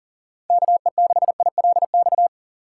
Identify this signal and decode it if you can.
Morse code, heard as a keyed tone, about 40 wpm, 700 Hz. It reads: KE6ILX
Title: USA Call Signs - 3x Character Spacing - 40wpm